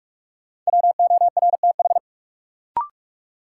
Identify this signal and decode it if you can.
Morse code, heard as a keyed tone, about 44 wpm, 700 Hz. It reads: WORTH
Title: Sentences from Top 800 Words - New Word Drill - 45wpm